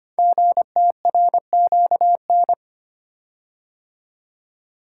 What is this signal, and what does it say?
Morse code, heard as a keyed tone, about 25 wpm, 700 Hz. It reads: GTRQN
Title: Code Groups - 5 Letters - 25wpm